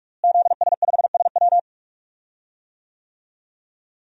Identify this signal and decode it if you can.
Morse code, heard as a keyed tone, about 46 wpm, 700 Hz. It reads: ZS5SW
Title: DXCC-100 Call Signs - 5 Characters - Repeated 2x - 45wpm